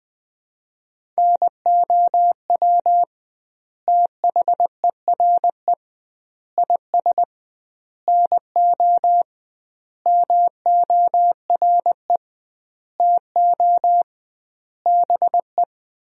Morse code, heard as a keyed tone, about 20 wpm, 700 Hz. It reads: NOW THERE IS NO MORE TO BE
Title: 2x Word-Spacing - Sentences from Top 100 Words - 20wpm